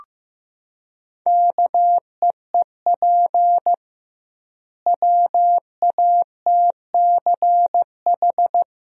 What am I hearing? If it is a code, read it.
Morse code, heard as a keyed tone, about 15 wpm, 700 Hz. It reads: KEEP WATCH
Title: 2x Word-Spacing - Sentences from Top 300 Words - 15wpm